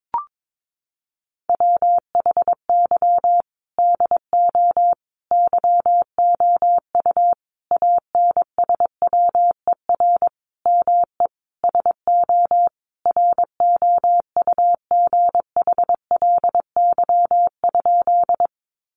Morse code, heard as a keyed tone, about 22 wpm, 700 Hz. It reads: WHY DO YOU ANSWER ME SO ROUGHLY?
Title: Sentences from Top 1500 Words 22wpm